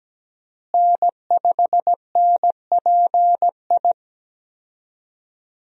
Morse code, heard as a keyed tone, about 17 wpm, 700 Hz. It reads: N5NPI
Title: USA Call Signs - 5 Characters - Repeated 2x - 17wpm